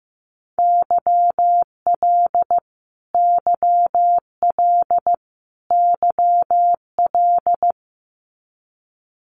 Morse code, heard as a keyed tone, about 15 wpm, 700 Hz. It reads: YL YL YL
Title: Abbreviations Repeated 3x - 15wpm